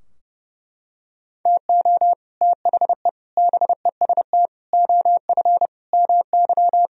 Morse code, heard as a keyed tone, about 32 wpm, 700 Hz. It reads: TO THE BEST OF MY
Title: Sentences from Top 1000 Words - Review Part 3 - 30wpm